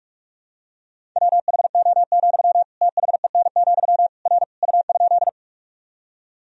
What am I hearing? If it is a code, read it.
Morse code, heard as a keyed tone, about 45 wpm, 700 Hz. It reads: WHO, THEN, R U?
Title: Sentences from Top 100 Words and Abbreviations 45wpm